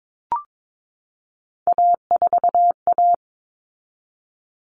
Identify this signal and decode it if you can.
Morse code, heard as a keyed tone, about 22 wpm, 700 Hz. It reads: A4A
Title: All Cut Numbers - 3-Digit 22wpm